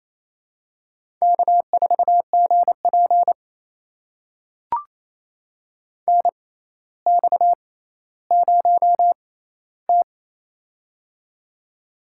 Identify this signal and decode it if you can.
Morse code, heard as a keyed tone, about 28 wpm, spaced as for 14 wpm, 700 Hz. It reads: K4GP NX0T
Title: USA Call Signs - 4 Characters - 6x Character Spacing - 28wpm